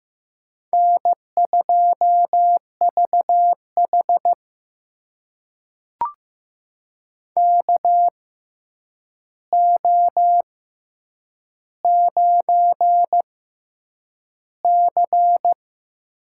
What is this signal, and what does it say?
Morse code, heard as a keyed tone, about 15 wpm, spaced as for 8 wpm, 700 Hz. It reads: N2VH KO9C
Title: USA Call Signs - 4 Characters - 6x Character Spacing - 15wpm